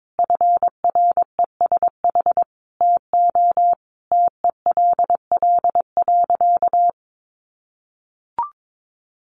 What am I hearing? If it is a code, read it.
Morse code, heard as a keyed tone, about 22 wpm, 700 Hz. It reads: FRESH TO TELL.
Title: Sentences from Top 1200 Words 22wpm